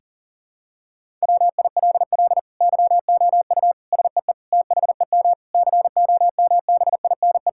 Morse code, heard as a keyed tone, about 37 wpm, 700 Hz. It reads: WILL YOU SEE THEM COMBINE
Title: Sentences from Top 1600 Words 40wpm